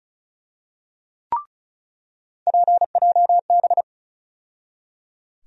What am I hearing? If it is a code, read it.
Morse code, heard as a keyed tone, about 36 wpm, 700 Hz. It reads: PJB
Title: USA Call Sign Postfixes - 3 Letters 35wpm